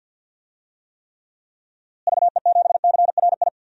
Morse code, heard as a keyed tone, about 46 wpm, 700 Hz. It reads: VE7CRI
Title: DXCC-100 Call Signs - 6 Characters - Repeated 3x - 50wpm